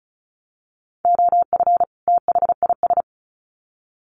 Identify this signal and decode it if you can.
Morse code, heard as a keyed tone, about 34 wpm, 700 Hz. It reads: OF THIS
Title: Sets of 2 Words 35wpm